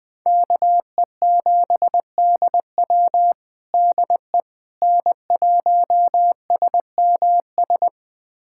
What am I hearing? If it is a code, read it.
Morse code, heard as a keyed tone, about 21 wpm, 700 Hz. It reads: KE7DW DE N1SMS
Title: USA Callsign DE Callsign - 5 Characters - 20wpm